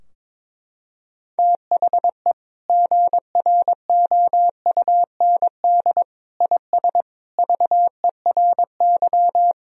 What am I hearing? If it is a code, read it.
Morse code, heard as a keyed tone, about 22 wpm, 700 Hz. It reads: THE GROUND IS VERY
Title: Sentences from Top 1000 Words - Review Part 2 - 22wpm